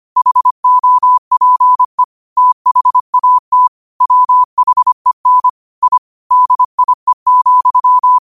Morse code, heard as a keyed tone, about 25 wpm, 1000 Hz. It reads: HOPE THAT WHEN I DIE,
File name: qotdfiles / 25wpm